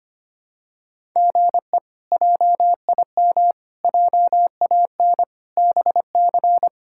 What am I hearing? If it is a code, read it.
Morse code, heard as a keyed tone, about 25 wpm, 700 Hz. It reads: GE JIM JAN BC
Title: K1USN SST QSO Practice - 25wpm